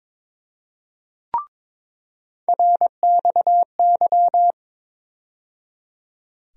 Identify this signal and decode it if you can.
Morse code, heard as a keyed tone, about 22 wpm, 700 Hz. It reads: RXY
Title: USA Call Sign Postfixes - 3 Letters 22wpm